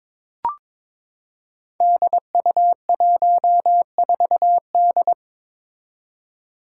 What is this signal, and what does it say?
Morse code, heard as a keyed tone, about 21 wpm, 700 Hz. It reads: DU14D
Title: Five Challenging Letter-Number Combinations 22wpm